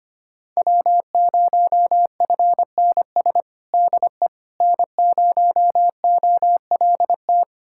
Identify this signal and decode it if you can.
Morse code, heard as a keyed tone, about 25 wpm, 700 Hz. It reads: W0FNS DE N0OLT
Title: USA Callsign DE Callsign - 5 Characters - 25wpm